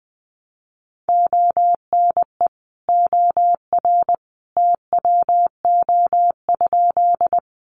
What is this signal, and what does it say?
Morse code, heard as a keyed tone, about 20 wpm, 700 Hz. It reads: ONE OR TWO?
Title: Sentences from Top 100 Words 20wpm